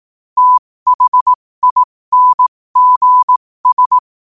Morse code, heard as a keyed tone, about 18 wpm, spaced as for 15 wpm, 1000 Hz. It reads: THINGS
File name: qotdfiles / 15wpm